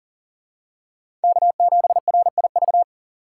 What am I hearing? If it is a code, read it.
Morse code, heard as a keyed tone, about 40 wpm, 700 Hz. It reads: K7RIV
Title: USA Call Signs - 5 Characters - 2x Character Spacing - 40wpm